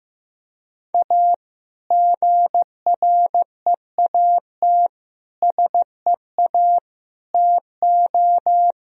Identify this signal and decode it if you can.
Morse code, heard as a keyed tone, about 16 wpm, 700 Hz. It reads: A GREAT SEA TO
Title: Sentences from Top 1000 Words - Review Part 2 - 15wpm